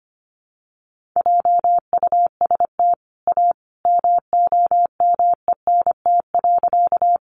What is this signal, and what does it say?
Morse code, heard as a keyed tone, about 25 wpm, 700 Hz. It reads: JUST A MOMENT.
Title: Sentences from Top 1000 Words 25wpm